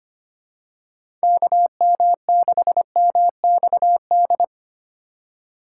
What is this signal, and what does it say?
Morse code, heard as a keyed tone, about 25 wpm, 700 Hz. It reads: KM6MXD
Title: USA Call Signs - 6 Characters - 4x Character Spacing - 25wpm